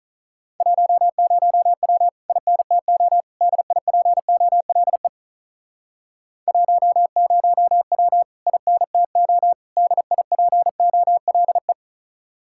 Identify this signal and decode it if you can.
Morse code, heard as a keyed tone, about 47 wpm, 700 Hz. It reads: 10W INTO DIPOLE 10W INTO DIPOLE